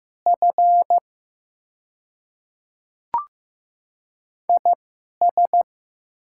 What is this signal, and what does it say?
Morse code, heard as a keyed tone, about 15 wpm, spaced as for 11 wpm, 700 Hz. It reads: F IS
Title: 2x Character Spacing - Top 300 Words - 15wpm